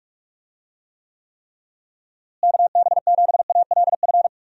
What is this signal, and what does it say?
Morse code, heard as a keyed tone, about 46 wpm, 700 Hz. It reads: KB7ALF
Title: USA Call Signs - 6 Characters - Repeated 3x - 45wpm